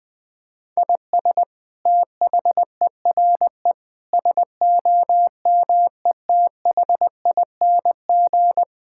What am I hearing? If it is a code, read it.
Morse code, heard as a keyed tone, about 20 wpm, 700 Hz. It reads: IS THERE SOMETHING
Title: Sentences from Top 1000 Words - Review Part 6 - 20wpm